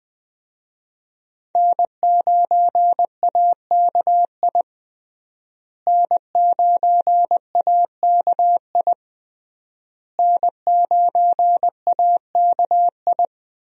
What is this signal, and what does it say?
Morse code, heard as a keyed tone, about 21 wpm, 700 Hz. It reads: N9AKI N9AKI N9AKI
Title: USA Call Signs - 5 Characters - Repeated 3x - 20wpm